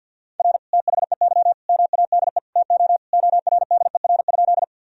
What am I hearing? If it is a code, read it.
Morse code, heard as a keyed tone, about 51 wpm, 700 Hz. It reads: R THEY MADE TO ORDER?
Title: Sentences from Top 400 Words and Abbreviations 50wpm